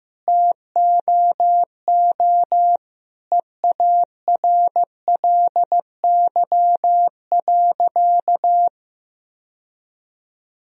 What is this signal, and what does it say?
Morse code, heard as a keyed tone, about 15 wpm, 700 Hz. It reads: TOO EARLY.
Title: Sentences from Top 1000 Words - Review Part 2 - 15wpm